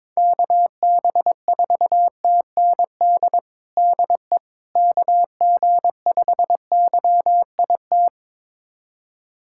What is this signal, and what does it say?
Morse code, heard as a keyed tone, about 22 wpm, 700 Hz. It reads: KB4TND DE KG5YIT
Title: USA Callsign DE Callsign - 6 Characters - 22wpm